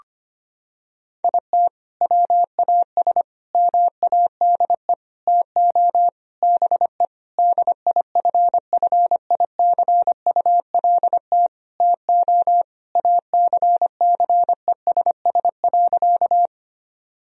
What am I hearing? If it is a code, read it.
Morse code, heard as a keyed tone, about 25 wpm, 700 Hz. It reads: IT WAS MADE TO BE DIFFICULT TO ACCESS.
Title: Sentences from Top 1000 Words - Review Part 4 - 25wpm